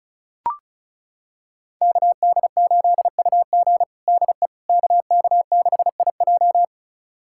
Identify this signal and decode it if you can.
Morse code, heard as a keyed tone, about 36 wpm, 700 Hz. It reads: KD8UG DE KK6IJ
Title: USA Callsign DE Callsign - 5 Characters - 35wpm